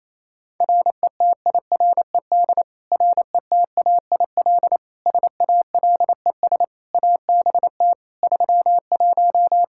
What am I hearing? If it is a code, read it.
Morse code, heard as a keyed tone, about 27 wpm, 700 Hz. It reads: RETIRED RETAIL SALES ABT 31
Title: Practice QSO - Exchange 3 - More Info - 28wpm